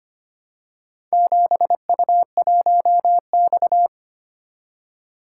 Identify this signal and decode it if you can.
Morse code, heard as a keyed tone, about 25 wpm, 700 Hz. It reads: 7U1X
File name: practice / 25wpm / Four Letter-Number Combinations - 25wpm